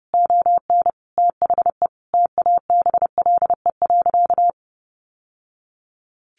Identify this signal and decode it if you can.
Morse code, heard as a keyed tone, about 28 wpm, 700 Hz. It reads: ON THE TABLE.
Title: Sentences from Top 1100 Words 30wpm